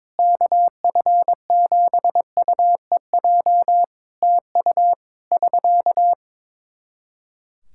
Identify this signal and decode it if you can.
Morse code, heard as a keyed tone, about 22 wpm, 700 Hz. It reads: KF7UEJ TU <SK>
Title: Practice QSO - Exchange 4 - Ending - 22wpm